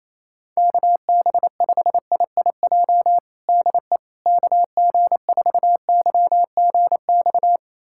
Morse code, heard as a keyed tone, about 27 wpm, 700 Hz. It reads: KB5IIJ DE KG4YGX
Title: USA Callsign DE Callsign - 6 Characters - 28wpm